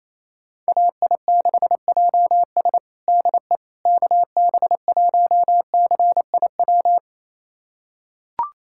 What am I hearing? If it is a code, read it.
Morse code, heard as a keyed tone, about 28 wpm, 700 Hz. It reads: AI6JS DE KB1CIW
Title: USA Callsign DE Callsign 28wpm